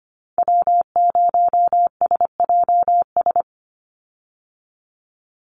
Morse code, heard as a keyed tone, about 25 wpm, 700 Hz. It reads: W0SJS
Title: Call Signs - 25wpm